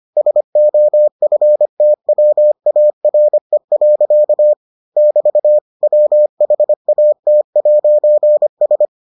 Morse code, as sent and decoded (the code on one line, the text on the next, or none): ... --- ..-. - .-- .- .-. .-.-.- / -...- / .-- .... .- - .----. ...
SOFTWARE. <BT> WHAT'S